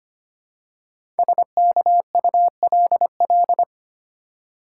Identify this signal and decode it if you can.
Morse code, heard as a keyed tone, about 25 wpm, 700 Hz. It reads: SKULL
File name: 5 Letter Words - First 3 Letters Given - 25wpm